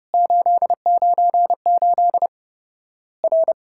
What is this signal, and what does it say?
Morse code, heard as a keyed tone, about 32 wpm, 700 Hz. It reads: 898 R
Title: Worked All Europe DX Contest - QTC Exchange - 30wpm